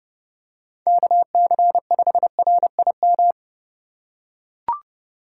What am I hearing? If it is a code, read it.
Morse code, heard as a keyed tone, about 30 wpm, 700 Hz. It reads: KC5RIM
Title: USA Call Signs - 3x Character Spacing - 30wpm